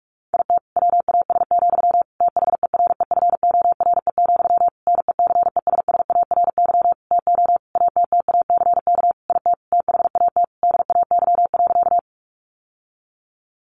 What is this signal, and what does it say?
Morse code, heard as a keyed tone, about 44 wpm, 700 Hz. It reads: IT WAS, THEREFORE, NECESSARY TO ATTACK IT THAT DAY.
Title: Sentences from Top 1500 Words 45wpm